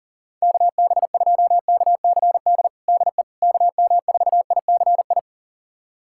Morse code, heard as a keyed tone, about 40 wpm, 700 Hz. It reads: KB2KCD DE KM4ICI
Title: USA Callsign DE Callsign 40wpm